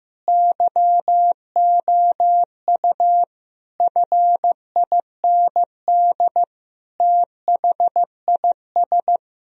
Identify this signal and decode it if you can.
Morse code, heard as a keyed tone, about 16 wpm, 700 Hz. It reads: YOU FIND THIS
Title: Sentences from Top 1000 Words - Review Part 2 - 15wpm